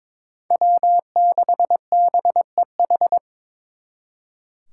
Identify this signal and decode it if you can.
Morse code, heard as a keyed tone, about 26 wpm, 700 Hz. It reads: W6BEH
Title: Speed Racing - USA 1x3 Call Signs 22wpm